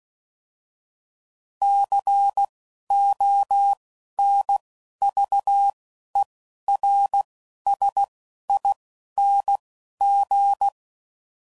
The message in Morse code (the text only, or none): C O N V E R S I N G